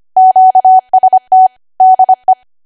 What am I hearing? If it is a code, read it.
Morse code, heard as a keyed tone, about 25 wpm, 750 Hz. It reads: QST DE
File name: Archive / 25 WPM / 25WPM